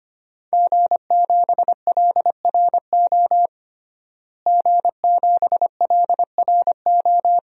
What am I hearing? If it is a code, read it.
Morse code, heard as a keyed tone, about 25 wpm, 700 Hz. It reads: G7LRO G7LRO
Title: DXCC-100 Call Signs - 5 Characters - Repeated 3x - 25wpm